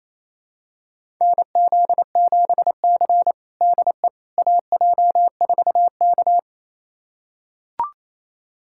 Morse code, heard as a keyed tone, about 28 wpm, 700 Hz. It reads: NZ7C DE AJ4K
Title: USA Callsign DE Callsign - 4 Characters - 28wpm